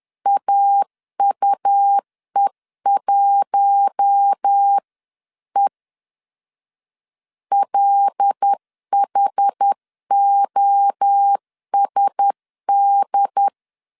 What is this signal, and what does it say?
Morse code, heard as a keyed tone, about 10 wpm, 800 Hz. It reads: AUE1E LHOSD
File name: CW / DET / 1 6 1500C S C 10wpm 4252